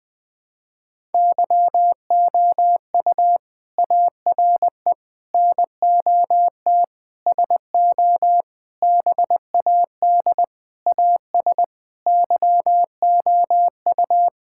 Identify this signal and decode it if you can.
Morse code, heard as a keyed tone, about 20 wpm, 700 Hz. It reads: YOU ARE NOT SO BAD AS YOU
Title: Sentences from Top 1000 Words - Review Part 6 - 20wpm